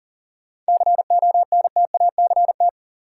Morse code, heard as a keyed tone, about 40 wpm, 700 Hz. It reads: CONTACT
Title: Sentences from Top 1000 Words - Review Part 7 - New Word Drill - 40wpm